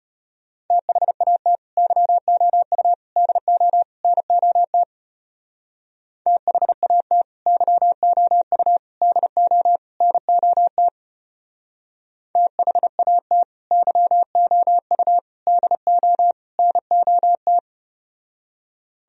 Morse code, heard as a keyed tone, about 32 wpm, 700 Hz. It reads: THAT YOU DO NOT THAT YOU DO NOT THAT YOU DO NOT